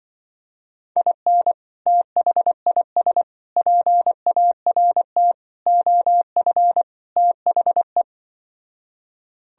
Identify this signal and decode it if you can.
Morse code, heard as a keyed tone, about 25 wpm, 700 Hz. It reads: IN THIS PART OF THE
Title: Speed-Racing - Sets of 5 Words 20wpm